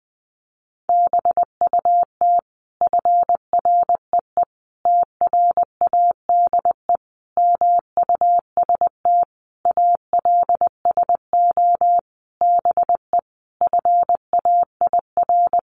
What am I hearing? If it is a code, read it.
Morse code, heard as a keyed tone, about 19 wpm, 700 Hz. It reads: BUT FREE TRADE MUST ALSO BE FAIR
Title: Sentences from Top 1300 Words 20wpm